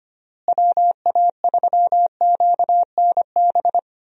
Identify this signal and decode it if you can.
Morse code, heard as a keyed tone, about 25 wpm, 700 Hz. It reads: WA3QNB
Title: USA Call Signs - 6 Characters - Repeated 2x - 25wpm